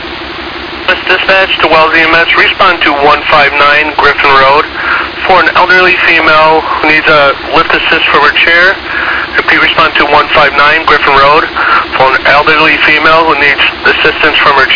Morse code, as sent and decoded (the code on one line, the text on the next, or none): none